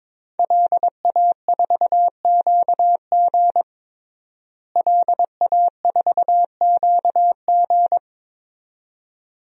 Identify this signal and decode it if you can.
Morse code, heard as a keyed tone, about 22 wpm, 700 Hz. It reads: LA4QG LA4QG
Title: DXCC-100 Call Signs - 5 Characters - Repeated 2x - 22wpm